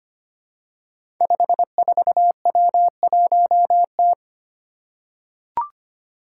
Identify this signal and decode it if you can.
Morse code, heard as a keyed tone, about 25 wpm, 700 Hz. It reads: 54W1T
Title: Five Challenging Letter-Number Combinations 25wpm